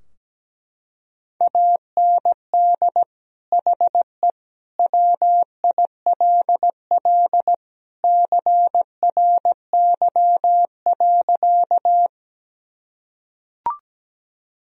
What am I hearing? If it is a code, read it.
Morse code, heard as a keyed tone, about 17 wpm, 700 Hz. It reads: AND HE WILL CRY.
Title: Sentences from Top 1000 Words - Review Part 2 - 17wpm